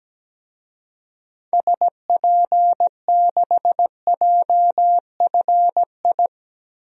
Morse code, heard as a keyed tone, about 17 wpm, 700 Hz. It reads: SP6JFI
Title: DXCC-100 Call Signs - 6 Characters - Repeated 2x - 17wpm